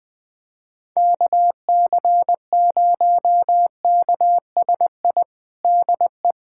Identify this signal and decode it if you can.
Morse code, heard as a keyed tone, about 20 wpm, 700 Hz. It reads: KC0KSI DE
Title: USA Callsign DE Callsign - 6 Characters - 20wpm